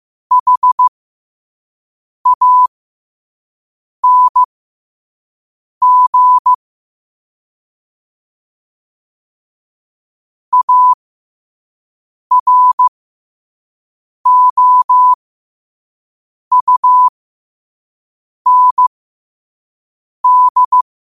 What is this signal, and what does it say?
Morse code, heard as a keyed tone, about 15 wpm, spaced as for 5 wpm, 1000 Hz. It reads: HANG AROUND
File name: qotdfiles / 5wpm